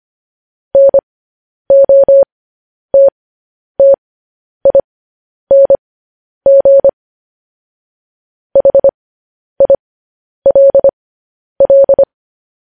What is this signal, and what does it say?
Morse code, heard as a keyed tone, about 25 wpm, spaced as for 10 wpm, 550 Hz. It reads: NOTTING HILL
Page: news-headlines-in-morse-code-at-10-wpm